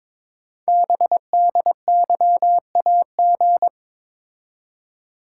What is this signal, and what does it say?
Morse code, heard as a keyed tone, about 22 wpm, 700 Hz. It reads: BDYAG
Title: Code Groups - 5 Letters - 22wpm